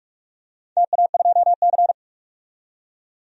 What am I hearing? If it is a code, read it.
Morse code, heard as a keyed tone, about 46 wpm, 700 Hz. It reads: TA2C
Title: DXCC-100 Call Signs - 4 Characters - 45wpm